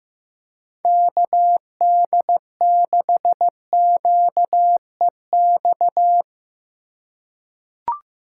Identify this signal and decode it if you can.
Morse code, heard as a keyed tone, about 15 wpm, 700 Hz. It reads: KD6QEX
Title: USA Call Signs - 3x Character Spacing - 15wpm